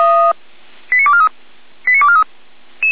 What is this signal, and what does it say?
Five-tone selective call (ZVEI), heard as quick runs of five tones, tones of about 70 ms. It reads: 89243 89243